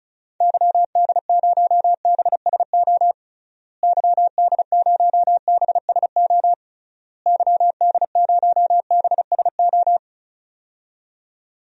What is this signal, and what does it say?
Morse code, heard as a keyed tone, about 36 wpm, 700 Hz. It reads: YD0BSO YD0BSO YD0BSO
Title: DXCC-100 Call Signs - 6 Characters - Repeated 3x - 35wpm